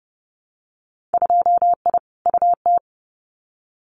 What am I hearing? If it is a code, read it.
Morse code, heard as a keyed tone, about 30 wpm, 700 Hz. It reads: 2I UT
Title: Winter Field Day Exchanges 30wpm